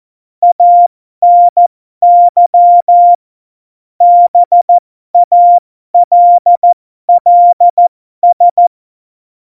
Morse code, heard as a keyed tone, about 14 wpm, spaced as for 12 wpm, 700 Hz. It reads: ANY BALLS
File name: Morsekurs / Witze